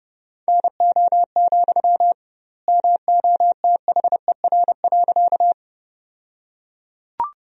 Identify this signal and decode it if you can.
Morse code, heard as a keyed tone, about 29 wpm, 700 Hz. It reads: NO, MOTHER.
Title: 2x Word-Spacing - Sentences from Top 200 Words - 30wpm